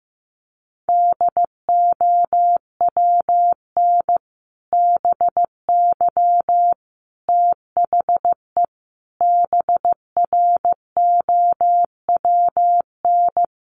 Morse code, heard as a keyed tone, about 15 wpm, 700 Hz. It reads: DOWN BY THE BROWN
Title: Sentences from Top 1200 Words 15wpm